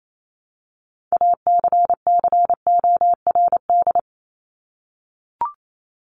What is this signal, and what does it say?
Morse code, heard as a keyed tone, about 28 wpm, 700 Hz. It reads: ACCORD
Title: ING Suffix - Top 100 Words with Base Words 28wpm